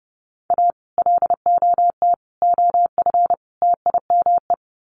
Morse code, heard as a keyed tone, about 30 wpm, 700 Hz. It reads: A LOT OF TIME
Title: Sets of 4 Words 30wpm